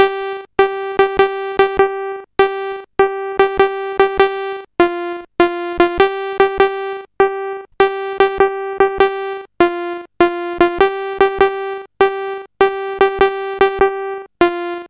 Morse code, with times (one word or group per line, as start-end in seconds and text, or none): none